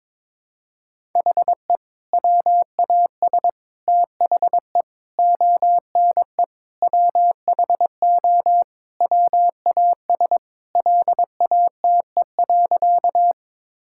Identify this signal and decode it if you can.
Morse code, heard as a keyed tone, about 22 wpm, 700 Hz. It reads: HE WAS THE ONE WHO WAS LATE.
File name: practice / 22wpm / Sentences with 2-Word Phrases - 22wpm